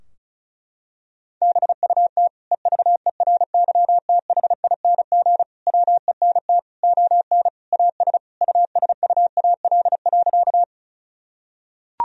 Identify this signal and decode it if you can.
Morse code, heard as a keyed tone, about 36 wpm, 700 Hz. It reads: BUT EVERYTHING WENT ON AS USUAL.
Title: Sentences from Top 1000 Words - Review Part 2 - 35wpm